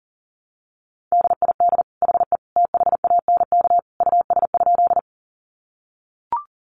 Morse code, heard as a keyed tone, about 37 wpm, 700 Hz. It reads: DID HE THANK US?
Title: Sentences from Top 600 Words 40wpm